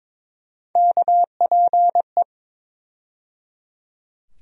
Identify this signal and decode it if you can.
Morse code, heard as a keyed tone, about 23 wpm, 700 Hz. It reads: KPE
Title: USA Call Sign Postfixes - 3 Letters 22wpm